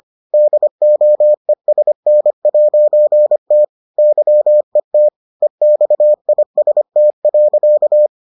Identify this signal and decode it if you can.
Morse code, heard as a keyed tone, about 25 wpm, 600 Hz. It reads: DOESN'T YET EXIST.